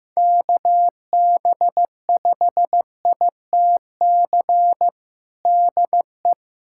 Morse code, heard as a keyed tone, about 15 wpm, 700 Hz. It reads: KB5ITC DE
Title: USA Callsign DE Callsign - 6 Characters - 15wpm